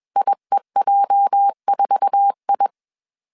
Morse code, heard as a keyed tone, about 21 wpm, 800 Hz. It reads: IEJ4I